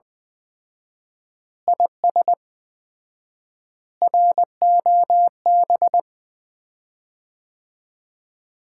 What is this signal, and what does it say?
Morse code, heard as a keyed tone, about 20 wpm, 700 Hz. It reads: IS ROB
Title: Common QSO Phrases - 4x Word-Spacing - 20wpm